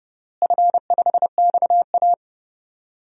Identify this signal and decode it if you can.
Morse code, heard as a keyed tone, about 30 wpm, 700 Hz. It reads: F5XA
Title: DXCC-100 Call Signs - 4 Characters - Repeated 3x - 30wpm